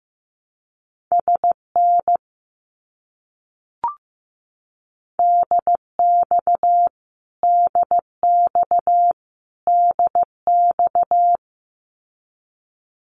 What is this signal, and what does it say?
Morse code, heard as a keyed tone, about 15 wpm, 700 Hz. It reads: SN DX DX DX
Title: Abbreviations Repeated 3x - 15wpm